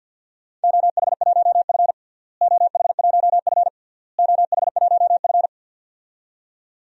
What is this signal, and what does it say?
Morse code, heard as a keyed tone, about 51 wpm, 700 Hz. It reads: OH1F OH1F OH1F
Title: DXCC-100 Call Signs - 4 Characters - Repeated 3x - 50wpm